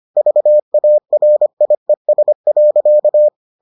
Morse code, heard as a keyed tone, about 25 wpm, 600 Hz. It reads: VARIES.